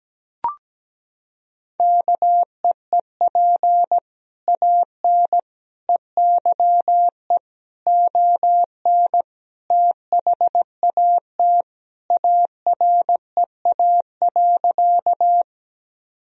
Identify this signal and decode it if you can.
Morse code, heard as a keyed tone, about 17 wpm, 700 Hz. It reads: KEEP AN EYE ON THAT AREA.
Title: Sentences from Top 1000 Words - Review Part 2 - 17wpm